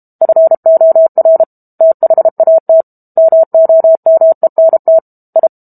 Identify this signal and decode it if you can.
Morse code, heard as a keyed tone, about 32 wpm, 650 Hz. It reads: FOR THAT MOMENT I
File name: Book Two / 06-Chapter 6 - 32 WPM